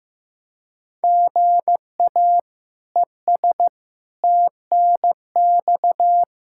Morse code, heard as a keyed tone, about 15 wpm, 700 Hz. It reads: GA ES TNX
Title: Practice QSO - Exchange 1 - Beginning - 15wpm